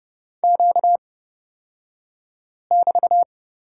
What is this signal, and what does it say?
Morse code, heard as a keyed tone, about 31 wpm, spaced as for 5 wpm, 700 Hz. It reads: Q<BT>
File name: Lesson 38d - Letter q - Two Characters 30wpm